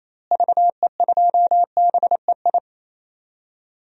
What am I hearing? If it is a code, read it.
Morse code, heard as a keyed tone, about 27 wpm, 700 Hz. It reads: VE2BEI
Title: DXCC-100 Call Signs - 6 Characters - Repeated 3x - 28wpm